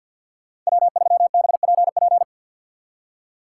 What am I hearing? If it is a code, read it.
Morse code, heard as a keyed tone, about 62 wpm, 700 Hz. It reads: W3BPP
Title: Speed Racing - USA Call Signs 50wpm